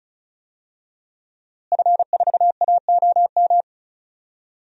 Morse code, heard as a keyed tone, about 36 wpm, 700 Hz. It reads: F4AOM
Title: DXCC-100 Call Signs - 5 Characters - Repeated 2x - 35wpm